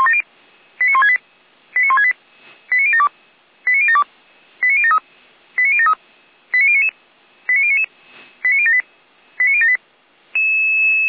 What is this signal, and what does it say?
Five-tone selective call (ZVEI), heard as quick runs of five tones, tones of about 70 ms. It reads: 89168 89168 89072 89072 89073 89073 89099 89099 89078 89078